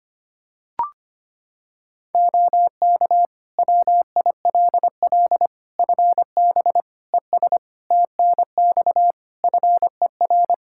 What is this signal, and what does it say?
Morse code, heard as a keyed tone, about 25 wpm, 700 Hz. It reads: OK WILL FB ES TNX FER
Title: Practice QSO - Exchange 2 - Info - 25wpm